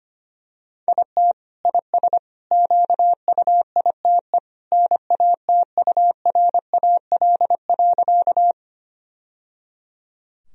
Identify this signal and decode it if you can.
Morse code, heard as a keyed tone, about 25 wpm, 700 Hz. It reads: IT IS QUITE NATURAL.
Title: Sentences from Top 1000 Words - Review Part 4 - 25wpm